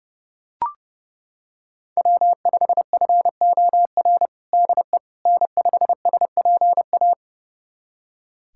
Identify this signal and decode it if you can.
Morse code, heard as a keyed tone, about 30 wpm, 700 Hz. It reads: W5FOR DE N5SPA
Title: USA Callsign DE Callsign - 5 Characters - 30wpm